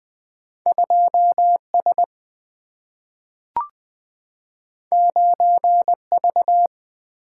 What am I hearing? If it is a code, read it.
Morse code, heard as a keyed tone, about 20 wpm, 700 Hz. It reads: 2S 9V
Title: Single-Letter Single-Number 20wpm